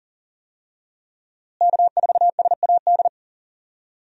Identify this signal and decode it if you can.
Morse code, heard as a keyed tone, about 40 wpm, 700 Hz. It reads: K4SAD
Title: USA Call Signs - 5 Characters - Repeated 3x - 40wpm